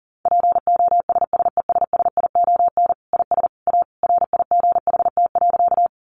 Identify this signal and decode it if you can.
Morse code, heard as a keyed tone, about 40 wpm, 700 Hz. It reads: POSSESSION IS A RIGHT.
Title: Sentences from Top 1400 Words 40wpm